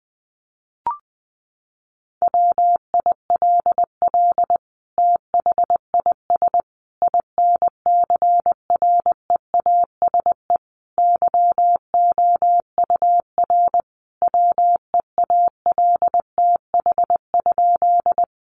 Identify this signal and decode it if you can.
Morse code, heard as a keyed tone, about 20 wpm, 700 Hz. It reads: WILL THIS INCREASE YOUR WEALTH?
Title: Sentences from Top 1200 Words 20wpm